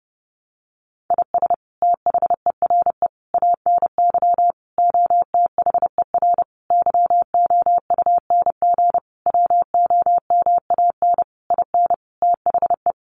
Morse code, heard as a keyed tone, about 30 wpm, 700 Hz. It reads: IS THERE ANY OTHER YOUNG WOMAN IN THE
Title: Sentences from Top 1100 Words 30wpm